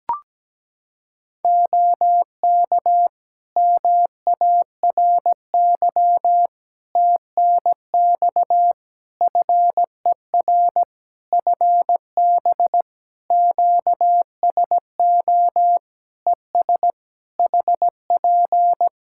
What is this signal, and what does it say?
Morse code, heard as a keyed tone, about 17 wpm, 700 Hz. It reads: OK MARY TNX FER FB QSO ES HP
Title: Practice QSO - Exchange 4 - Ending - 17wpm